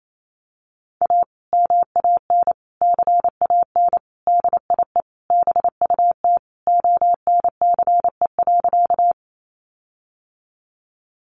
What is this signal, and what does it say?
Morse code, heard as a keyed tone, about 28 wpm, 700 Hz. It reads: A MAN CAN DIE BUT ONCE.
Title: Sentences from Top 1300 Words 28wpm